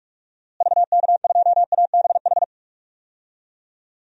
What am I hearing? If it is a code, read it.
Morse code, heard as a keyed tone, about 41 wpm, 700 Hz. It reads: VK2ABH
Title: DXCC-100 Call Signs - 6 Characters - Repeated 2x - 45wpm